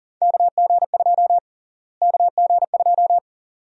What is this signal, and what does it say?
Morse code, heard as a keyed tone, about 40 wpm, 700 Hz. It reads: KG2 KG2
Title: USA Call Sign Prefixes - Repeated 2x - 40wpm